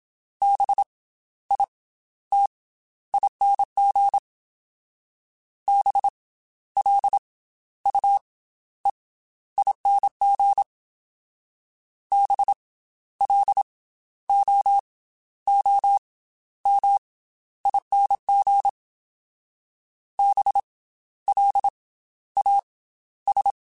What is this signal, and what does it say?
Morse code, heard as a keyed tone, about 27 wpm, 800 Hz. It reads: B I T ING B L U E ING B L O O M ING B L A S
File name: Special INGs / Special INGs-mp3s / Slow ing @25